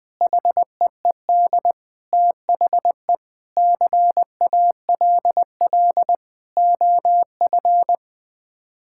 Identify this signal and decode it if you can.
Morse code, heard as a keyed tone, about 20 wpm, 700 Hz. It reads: HEED THE CALL OF
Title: Sets of 8 Words - 20wpm